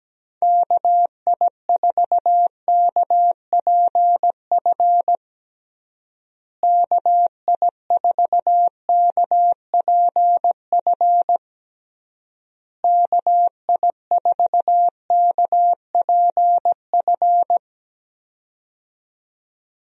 Morse code, heard as a keyed tone, about 17 wpm, 700 Hz. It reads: KI4KPF KI4KPF KI4KPF
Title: USA Call Signs - 6 Characters - Repeated 3x - 17wpm